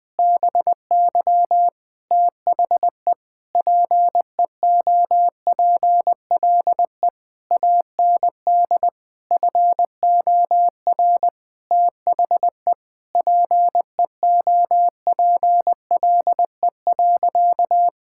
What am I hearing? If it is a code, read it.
Morse code, heard as a keyed tone, about 20 wpm, 700 Hz. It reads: BY THE PEOPLE AND FOR THE PEOPLE.
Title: Sentences with 8-Word Phrases - 20wpm